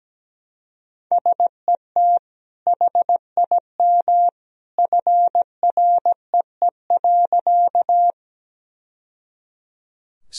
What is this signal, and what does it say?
Morse code, heard as a keyed tone, about 17 wpm, 700 Hz. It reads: SET HIM FREE.
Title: Sentences from Top 1000 Words - Review Part 1 - 17wpm